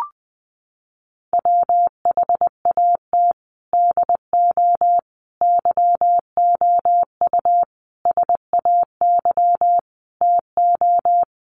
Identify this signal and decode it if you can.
Morse code, heard as a keyed tone, about 20 wpm, 700 Hz. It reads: WHAT DO YOU SAY TO
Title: Sets of 5 Words 20wpm